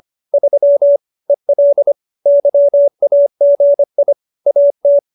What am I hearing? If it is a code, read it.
Morse code, heard as a keyed tone, about 25 wpm, 570 Hz. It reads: 3 EL YAGI AT